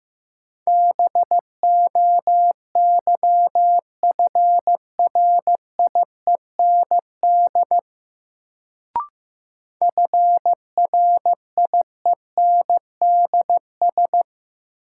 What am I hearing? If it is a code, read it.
Morse code, heard as a keyed tone, about 15 wpm, 700 Hz. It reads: BOYFRIEND FRIENDS
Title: Sentences from Top 1000 Words - Review Part 5 - New Word Drill - 15wpm